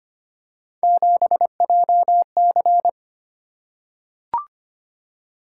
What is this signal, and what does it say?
Morse code, heard as a keyed tone, about 25 wpm, 700 Hz. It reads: 7JC
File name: Three Letter-Number Combinations - 25wpm